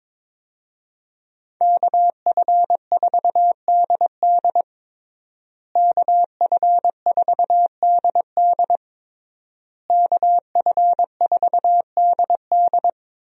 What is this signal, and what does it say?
Morse code, heard as a keyed tone, about 22 wpm, 700 Hz. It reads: KF4DD KF4DD KF4DD
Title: USA Call Signs - 5 Characters - Repeated 3x - 22wpm